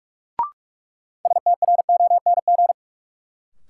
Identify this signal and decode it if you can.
Morse code, heard as a keyed tone, about 46 wpm, 700 Hz. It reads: STRONG
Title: Sentences from Top 500 Words - New Word Drill - 45wpm